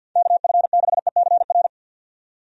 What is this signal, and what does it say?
Morse code, heard as a keyed tone, about 53 wpm, 700 Hz. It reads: KF6ECR
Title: USA Call Signs - 6 Characters - 50wpm